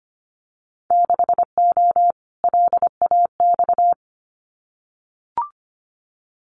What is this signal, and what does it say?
Morse code, heard as a keyed tone, about 25 wpm, 700 Hz. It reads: 6O LAX
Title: Winter Field Day Exchanges Encore 25wpm